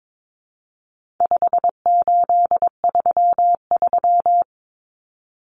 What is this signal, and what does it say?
Morse code, heard as a keyed tone, about 22 wpm, 700 Hz. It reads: 5833
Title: Numbers - 4-Digit 22wpm